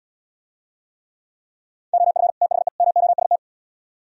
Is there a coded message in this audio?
Yes: Morse code from 1.9 to 3.4 s, heard as a keyed tone, about 77 wpm, 700 Hz. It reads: OF THE MOST